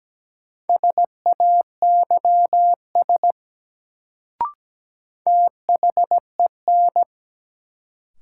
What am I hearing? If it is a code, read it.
Morse code, heard as a keyed tone, about 17 wpm, 700 Hz. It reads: SAYS THEN
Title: Sets of 2 Words - New Word Drill - 17wpm